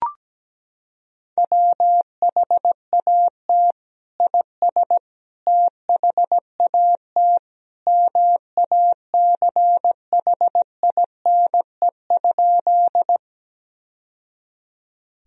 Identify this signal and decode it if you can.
Morse code, heard as a keyed tone, about 17 wpm, 700 Hz. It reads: WHAT IS THAT MACHINE?
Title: Sentences from Top 1000 Words - Review Part 2 - 17wpm